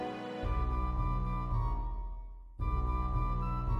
There is no code.